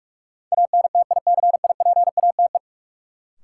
Morse code, heard as a keyed tone, about 46 wpm, 700 Hz. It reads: ANTICIPATE
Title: Sentences from Top 1500 Words - New Word Drill - 45wpm